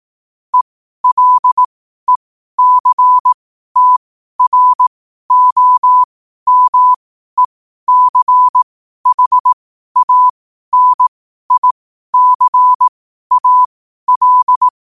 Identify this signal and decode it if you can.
Morse code, heard as a keyed tone, about 18 wpm, spaced as for 12 wpm, 1000 Hz. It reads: ELECTROMECHANICAL
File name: 12wpm